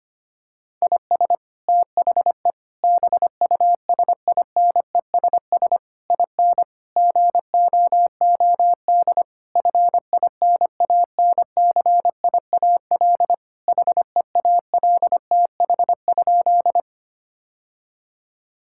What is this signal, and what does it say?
Morse code, heard as a keyed tone, about 25 wpm, 700 Hz. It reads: IS THE BUSINESS IN GOOD FINANCIAL HEALTH?
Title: Sentences from Top 1000 Words - Review Part 5 - 25wpm